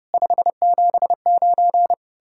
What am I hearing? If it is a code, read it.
Morse code, heard as a keyed tone, about 38 wpm, 700 Hz. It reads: 579